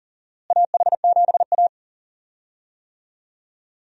Morse code, heard as a keyed tone, about 40 wpm, 700 Hz. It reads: AH7A
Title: USA Call Signs - 4 Characters - Repeated 2x - 40wpm